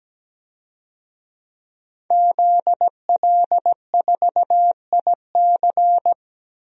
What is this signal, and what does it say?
Morse code, heard as a keyed tone, about 17 wpm, 700 Hz. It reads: ZL4IC
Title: DXCC-100 Call Signs - 5 Characters - 17wpm